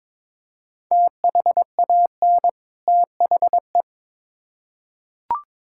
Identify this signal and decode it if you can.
Morse code, heard as a keyed tone, about 22 wpm, 700 Hz. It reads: THAN THE
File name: Build A Phrase - 2-Word Phrases - 22wpm